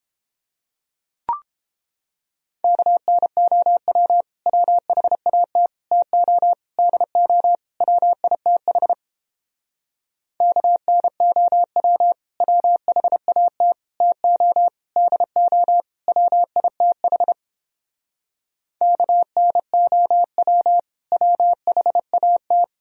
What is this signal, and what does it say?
Morse code, heard as a keyed tone, about 29 wpm, 700 Hz. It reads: KNOW WHAT TO DO WITH KNOW WHAT TO DO WITH KNOW WHAT